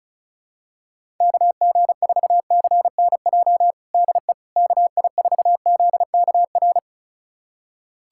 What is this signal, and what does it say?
Morse code, heard as a keyed tone, about 36 wpm, 700 Hz. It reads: KG4CNJ DE KI4ZKR
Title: USA Callsign DE Callsign - 6 Characters - 35wpm